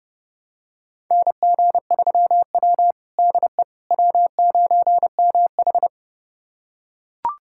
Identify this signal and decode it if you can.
Morse code, heard as a keyed tone, about 32 wpm, 700 Hz. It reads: NG3W DE W9MH
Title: USA Callsign DE Callsign - 4 Characters - 30wpm